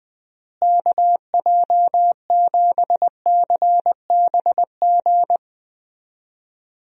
Morse code, heard as a keyed tone, about 21 wpm, 700 Hz. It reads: KJ7CBG
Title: USA Call Signs - 6 Characters - Repeated 2x - 20wpm